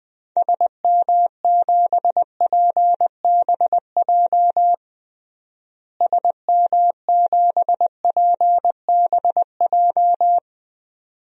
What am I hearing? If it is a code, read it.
Morse code, heard as a keyed tone, about 21 wpm, 700 Hz. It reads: SM7PBJ SM7PBJ
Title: DXCC-100 Call Signs - 6 Characters - Repeated 2x - 20wpm